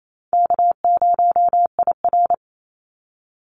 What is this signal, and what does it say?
Morse code, heard as a keyed tone, about 28 wpm, 700 Hz. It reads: K0IR
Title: Easy Call Signs Repeated 2x - 28wpm